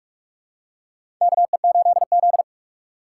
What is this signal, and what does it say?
Morse code, heard as a keyed tone, about 47 wpm, 700 Hz. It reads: KE8Z
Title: USA Call Signs - 4x Character Spacing - 45wpm